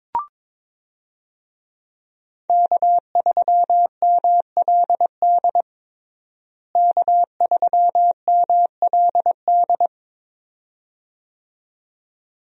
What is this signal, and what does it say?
Morse code, heard as a keyed tone, about 22 wpm, 700 Hz. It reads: K3MLD K3MLD
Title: USA Call Signs - 5 Characters - Repeated 2x - 22wpm